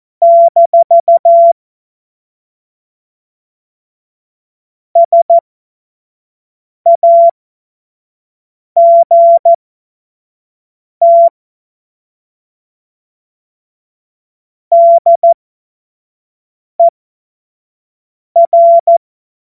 Morse code, heard as a keyed tone, about 14 wpm, spaced as for 5 wpm, 680 Hz. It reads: - SAGT DER
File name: Morsekurs / Witze / Witz 10. August 2021, 25 F70